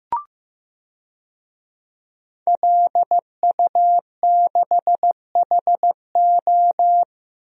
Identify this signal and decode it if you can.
Morse code, heard as a keyed tone, about 15 wpm, 700 Hz. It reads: LU6HO
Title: DXCC-100 Call Signs - 5 Characters - Repeated 3x - 15wpm